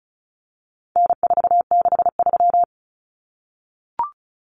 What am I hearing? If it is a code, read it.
Morse code, heard as a keyed tone, about 34 wpm, 700 Hz. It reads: N463
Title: Common Cut Numbers - 4-Digit 35wpm